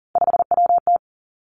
Morse code, heard as a keyed tone, about 40 wpm, 700 Hz. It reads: 5WT
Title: Three Challenging Letter-Number Combinations 40wpm